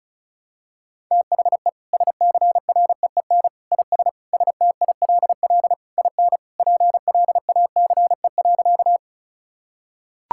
Morse code, heard as a keyed tone, about 36 wpm, 700 Hz. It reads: THE SCREEN IS STILL IN PLACE.
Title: Sentences from Top 1000 Words - Review Part 6 - 35wpm